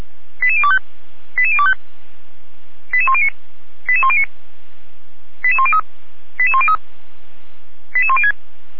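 Five-tone selective call (ZVEI), heard as quick runs of five tones, tones of about 70 ms. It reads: 88025 88025 88109 88109 88103 88103 88106
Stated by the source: ZVEI five-tone call 88192 received by SDR, from the Feuerwehr Orlamünde fire brigade audio archive.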